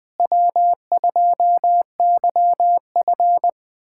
Morse code, heard as a keyed tone, about 21 wpm, 700 Hz. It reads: W2YF
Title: USA Callsign DE Callsign - 4 Characters - 20wpm